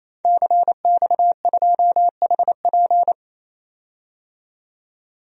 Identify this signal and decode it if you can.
Morse code, heard as a keyed tone, about 28 wpm, 700 Hz. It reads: CX2HP
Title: DXCC-100 Call Signs - 5 Characters - 28wpm